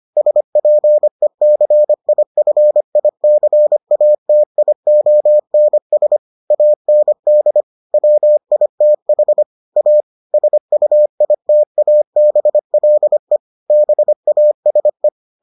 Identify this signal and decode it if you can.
Morse code, heard as a keyed tone, about 25 wpm, 600 Hz. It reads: SPECIFICATIONS AND WITH A SUITABLE BASE